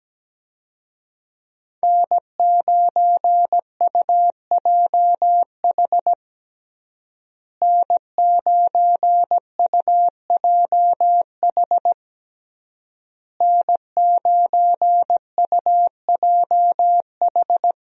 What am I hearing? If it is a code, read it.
Morse code, heard as a keyed tone, about 17 wpm, 700 Hz. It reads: N9UJH N9UJH N9UJH
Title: USA Call Signs - 5 Characters - Repeated 3x - 17wpm